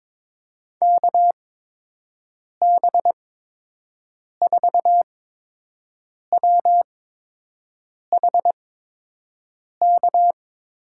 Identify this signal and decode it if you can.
Morse code, heard as a keyed tone, about 22 wpm, spaced as for 6 wpm, 700 Hz. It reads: KB4WHK
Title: USA Call Signs - 6 Characters - 8x Character Spacing - 22wpm